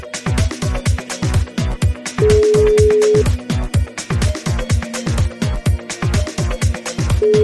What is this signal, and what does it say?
Ringback tone (Europe, 425 Hz), with noise.